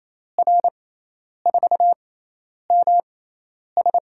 Morse code, heard as a keyed tone, about 28 wpm, spaced as for 10 wpm, 700 Hz. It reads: R4MS
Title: USA Call Signs - 6x Character Spacing - 28wpm